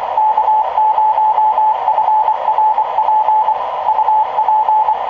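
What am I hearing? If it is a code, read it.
Morse code, heard as a keyed tone, about 24 wpm, 860 Hz, with radio noise. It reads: K0FRP UP
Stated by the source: QRP fox hunt: fox KØFRP (Al Dawkins, Colorado QRP Club #32) working hounds AA4LR and N0OUR.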